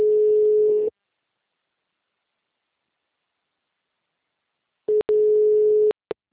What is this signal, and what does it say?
Ringback tone (Europe, 425 Hz).
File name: out- -2004- - - .25807